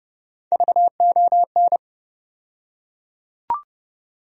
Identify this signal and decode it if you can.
Morse code, heard as a keyed tone, about 32 wpm, 700 Hz. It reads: VON
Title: USA Call Sign Postfixes - 3 Letters 30wpm